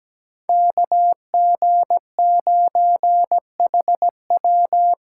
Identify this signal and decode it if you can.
Morse code, heard as a keyed tone, about 20 wpm, 700 Hz. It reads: KG9HW
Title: Speed Racing - USA 2x2 Call Signs 17wpm